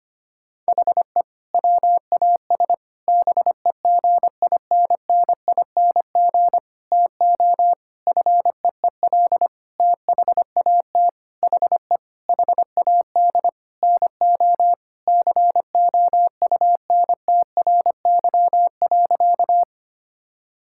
Morse code, heard as a keyed tone, about 25 wpm, 700 Hz. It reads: HE WAS BEGINNING TO FEEL THAT HE HAD NO COUNTRY.
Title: Sentences from Top 1000 Words - Review Part 5 - 25wpm